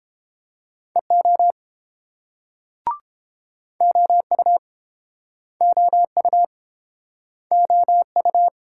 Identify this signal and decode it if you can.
Morse code, heard as a keyed tone, about 30 wpm, 690 Hz. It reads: EO OU OU OU